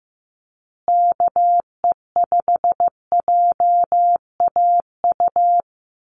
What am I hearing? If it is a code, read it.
Morse code, heard as a keyed tone, about 15 wpm, 700 Hz. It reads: KE5JAU
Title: USA Call Signs Repeated 2x 15wpm